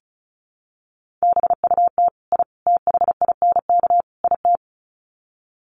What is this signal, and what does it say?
Morse code, heard as a keyed tone, about 34 wpm, 700 Hz. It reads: BUT I THINK IT
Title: Sets of 4 Words 35wpm